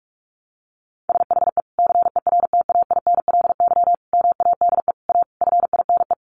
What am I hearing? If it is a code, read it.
Morse code, heard as a keyed tone, about 44 wpm, 700 Hz. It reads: SHE CERTAINLY MADE A FINE
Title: Sentences from Top 1100 Words 45wpm